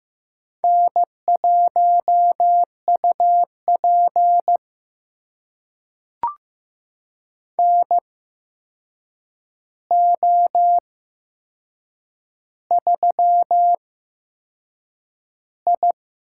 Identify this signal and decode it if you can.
Morse code, heard as a keyed tone, about 15 wpm, spaced as for 6 wpm, 700 Hz. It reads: N1UP NO3I
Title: USA Call Signs - 4 Characters - 8x Character Spacing - 15wpm